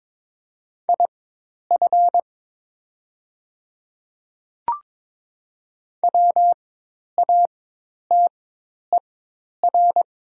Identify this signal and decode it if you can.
Morse code, heard as a keyed tone, about 22 wpm, spaced as for 10 wpm, 700 Hz. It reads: IF WATER
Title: 4x Character Spacing - Top 100 Words - 22wpm